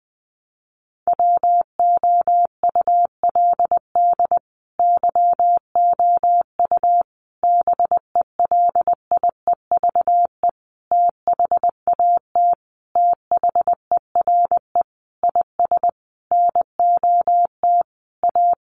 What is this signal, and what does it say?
Morse code, heard as a keyed tone, about 20 wpm, 700 Hz. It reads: WOULD YOU BELIEVE THAT THERE IS NOT A
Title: Sentences from Top 1300 Words 20wpm